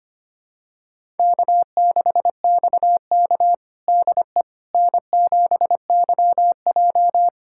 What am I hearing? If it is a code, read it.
Morse code, heard as a keyed tone, about 25 wpm, 700 Hz. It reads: K6XK DE N7YJ
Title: USA Callsign DE Callsign - 4 Characters - 25wpm